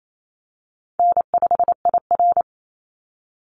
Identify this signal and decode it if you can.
Morse code, heard as a keyed tone, about 28 wpm, 700 Hz. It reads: N5IR
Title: Call Signs - 28wpm